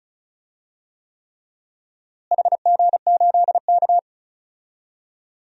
Morse code, heard as a keyed tone, about 36 wpm, 700 Hz. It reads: HG8K
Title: DXCC-100 Call Signs - 4 Characters - Repeated 3x - 35wpm